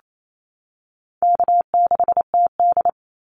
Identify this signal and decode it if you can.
Morse code, heard as a keyed tone, about 28 wpm, 700 Hz. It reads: K6TD
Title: Easy Call Signs - 28wpm